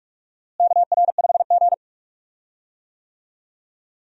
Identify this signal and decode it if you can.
Morse code, heard as a keyed tone, about 44 wpm, 700 Hz. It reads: KR5G
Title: USA Call Signs - 4 Characters - Repeated 2x - 45wpm